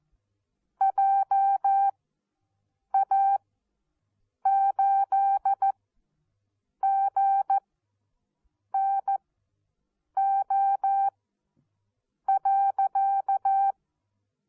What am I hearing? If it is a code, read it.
Morse code, heard as a keyed tone, about 14 wpm, spaced as for 6 wpm, 780 Hz. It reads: JA8GNO.
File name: Onlinelehrgang / morsen / DJ4UF Morsen14